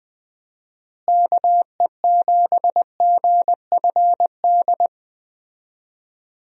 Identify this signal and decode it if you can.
Morse code, heard as a keyed tone, about 21 wpm, 700 Hz. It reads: KE7GFD
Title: USA Call Signs - 6 Characters - Repeated 3x - 20wpm